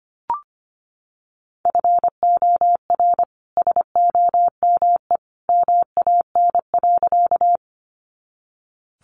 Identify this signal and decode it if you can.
Morse code, heard as a keyed tone, about 25 wpm, 700 Hz. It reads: FOR SOME MAN.
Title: Sentences from Top 200 Words 25wpm